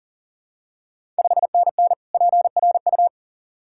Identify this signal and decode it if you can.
Morse code, heard as a keyed tone, about 40 wpm, 690 Hz. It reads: 5NN PRU
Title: Ontario QSO Party Exchange 40wpm